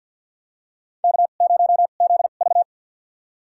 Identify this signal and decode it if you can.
Morse code, heard as a keyed tone, about 53 wpm, spaced as for 37 wpm, 700 Hz. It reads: K0ZV
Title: USA Call Signs - 4 Characters - 2x Character Spacing - 50wpm